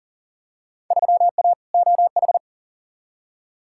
Morse code, heard as a keyed tone, about 40 wpm, 700 Hz. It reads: 3A OH
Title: Field Day Exchanges - 40wpm